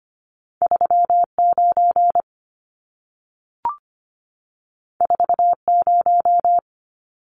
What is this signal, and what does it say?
Morse code, heard as a keyed tone, about 25 wpm, 700 Hz. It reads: 39 40
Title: Numbers - 2-Digit 25wpm